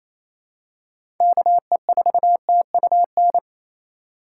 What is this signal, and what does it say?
Morse code, heard as a keyed tone, about 28 wpm, 700 Hz. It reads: KE4TUN
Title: USA Call Signs - 6 Characters - 3x Character Spacing - 28wpm